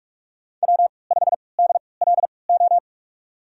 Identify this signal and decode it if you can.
Morse code, heard as a keyed tone, about 46 wpm, spaced as for 26 wpm, 700 Hz. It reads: W5DLO
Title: USA Call Signs - 3x Character Spacing - 45wpm